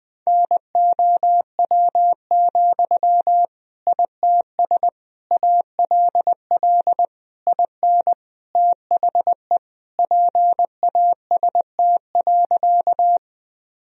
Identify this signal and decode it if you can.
Morse code, heard as a keyed tone, about 21 wpm, 700 Hz. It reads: NOW, ITS ALL IN THE PAST.
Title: Sentences from Top 1000 Words - Review Part 3 - 20wpm